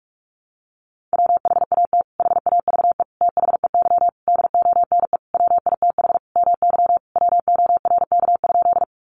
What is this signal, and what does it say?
Morse code, heard as a keyed tone, about 44 wpm, 700 Hz. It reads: WHAT HAVE THEY DONE WITH MY WORK?
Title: Sentences from Top 500 Words 45wpm